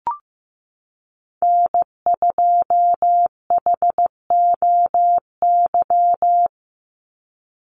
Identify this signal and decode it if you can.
Morse code, heard as a keyed tone, about 15 wpm, 700 Hz. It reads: N2HOY
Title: USA Call Signs 15wpm